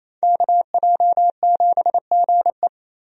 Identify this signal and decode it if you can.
Morse code, heard as a keyed tone, about 28 wpm, 700 Hz. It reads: KJ7GE
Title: USA Call Signs - 5 Characters - Repeated 3x - 28wpm